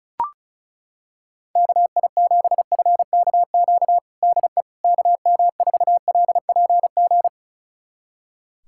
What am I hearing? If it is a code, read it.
Morse code, heard as a keyed tone, about 34 wpm, 690 Hz. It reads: KI7FKQ DE KM4LPG
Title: USA Callsign DE Callsign - 6 Characters - 35wpm